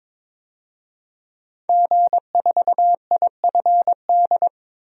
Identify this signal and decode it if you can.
Morse code, heard as a keyed tone, about 22 wpm, 700 Hz. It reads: G4IFD
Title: DXCC-100 Call Signs - 5 Characters - 22wpm